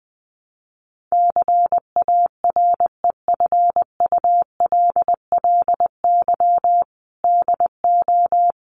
Morse code, heard as a keyed tone, about 19 wpm, 700 Hz. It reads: CAREFULLY DO
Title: Sentences from Top 1300 Words 20wpm